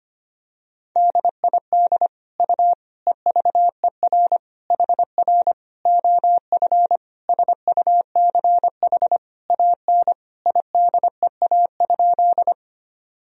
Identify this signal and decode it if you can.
Morse code, heard as a keyed tone, about 25 wpm, 700 Hz. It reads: DID U EVER HR OF SUCH AN IDEA?
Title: Sentences from Top 400 Words and Abbreviations 25wpm